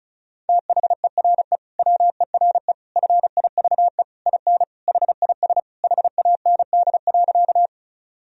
Morse code, heard as a keyed tone, about 36 wpm, 700 Hz. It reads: THERE WERE FIVE IN HIS HAND.
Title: Sentences from Top 1000 Words - Review Part 2 - 35wpm